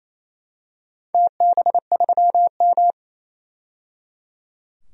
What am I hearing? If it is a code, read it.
Morse code, heard as a keyed tone, about 28 wpm, 700 Hz. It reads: TB3M
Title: DXCC-100 Call Signs - 4 Characters - 28wpm